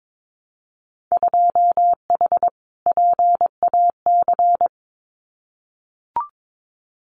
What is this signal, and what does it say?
Morse code, heard as a keyed tone, about 22 wpm, 700 Hz. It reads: 2H PAC
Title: Winter Field Day Exchanges Encore 22wpm